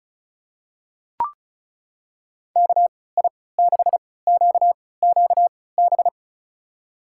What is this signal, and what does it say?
Morse code, heard as a keyed tone, about 36 wpm, spaced as for 20 wpm, 700 Hz. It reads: KI6QQB
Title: USA Call Signs - 6 Characters - 3x Character Spacing - 35wpm